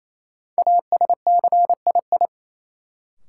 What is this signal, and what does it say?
Morse code, heard as a keyed tone, about 28 wpm, 700 Hz. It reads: ASCII
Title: Sets of 5 Words - New Word Drill - 28wpm